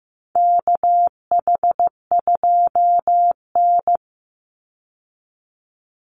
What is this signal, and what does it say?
Morse code, heard as a keyed tone, about 15 wpm, 700 Hz. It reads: KH2N
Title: Call Signs Repeated 3x - 15wpm